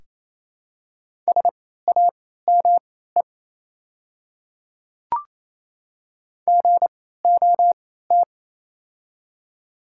Morse code, heard as a keyed tone, about 28 wpm, spaced as for 16 wpm, 700 Hz. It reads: SAME GOT
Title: 3x Character Spacing - Top 300 Words - 28wpm